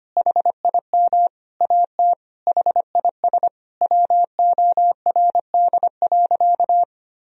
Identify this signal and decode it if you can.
Morse code, heard as a keyed tone, about 25 wpm, 700 Hz. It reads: HIM AT HIS WORD.
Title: Sentences from Top 1000 Words - Review Part 1 - 25wpm